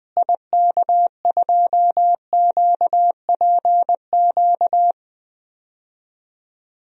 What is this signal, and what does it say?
Morse code, heard as a keyed tone, about 20 wpm, 700 Hz. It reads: IK2QPQ
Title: DXCC-100 Call Signs - 6 Characters - Repeated 2x - 20wpm